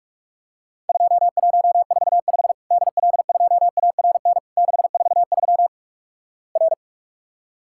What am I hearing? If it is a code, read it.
Morse code, heard as a keyed tone, about 46 wpm, 700 Hz. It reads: 2145 DL2ARN 643 R
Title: Worked All Europe DX Contest - QTC Exchange - 45wpm